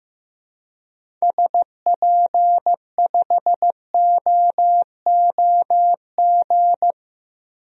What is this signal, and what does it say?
Morse code, heard as a keyed tone, about 15 wpm, 700 Hz. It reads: SP5OOG
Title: DXCC-100 Call Signs - 6 Characters - Repeated 3x - 15wpm